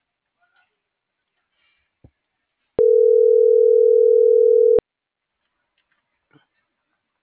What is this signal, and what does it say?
Ringback tone (North America), 440 + 480 Hz.